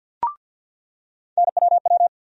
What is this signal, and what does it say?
Morse code, heard as a keyed tone, about 47 wpm, 700 Hz. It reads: NWW
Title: USA Call Sign Postfixes - 3 Letters 50wpm